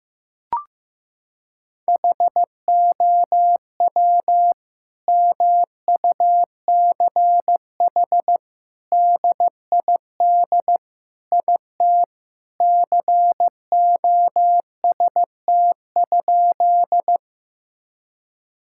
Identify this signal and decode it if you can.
Morse code, heard as a keyed tone, about 15 wpm, 700 Hz. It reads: HOW MUCH DID IT COST?
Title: Sentences from Top 1000 Words - Review Part 5 - 15wpm